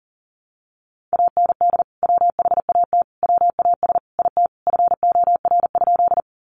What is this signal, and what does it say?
Morse code, heard as a keyed tone, about 40 wpm, 700 Hz. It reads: AND WHAT WAS IT FOR?
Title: Sentences from Top 100 Words 40wpm